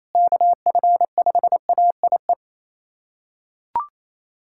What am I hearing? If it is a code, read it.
Morse code, heard as a keyed tone, about 27 wpm, 700 Hz. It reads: KF5AIE
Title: USA Callsign DE Callsign - 6 Characters - 28wpm